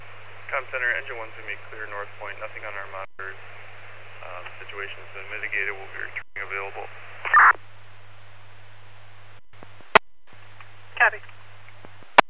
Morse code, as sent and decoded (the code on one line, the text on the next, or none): none